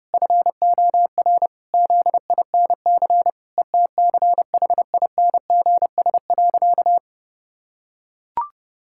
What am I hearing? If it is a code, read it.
Morse code, heard as a keyed tone, about 30 wpm, 700 Hz. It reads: FOR ZINC ETCHINGS.
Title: Pangram Challenge v2 - 30wpm